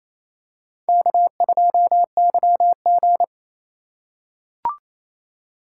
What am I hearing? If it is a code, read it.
Morse code, heard as a keyed tone, about 27 wpm, 700 Hz. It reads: K2YG
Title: USA Call Signs - 4 Characters - 4x Character Spacing - 28wpm